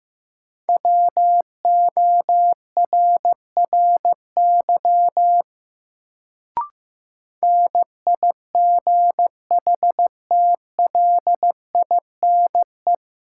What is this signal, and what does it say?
Morse code, heard as a keyed tone, about 15 wpm, 700 Hz. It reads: WORRY NIGHTLINE
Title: Sets of 5 Words - New Word Drill - 15wpm